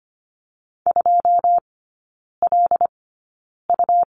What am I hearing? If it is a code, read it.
Morse code, heard as a keyed tone, about 25 wpm, spaced as for 9 wpm, 700 Hz. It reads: 2LU
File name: Lesson 17e - Letter u - Three Characters 25wpm